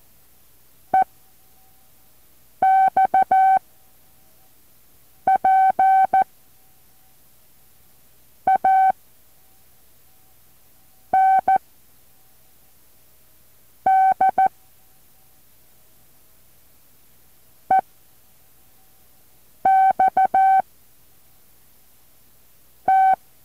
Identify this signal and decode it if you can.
Morse code, heard as a keyed tone, about 14 wpm, 760 Hz. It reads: E X P A N D E X T
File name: intl morse code2